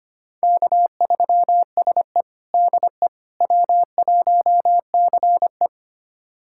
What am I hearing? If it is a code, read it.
Morse code, heard as a keyed tone, about 25 wpm, 700 Hz. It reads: K3SE DE W1CE
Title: USA Callsign DE Callsign - 4 Characters - 25wpm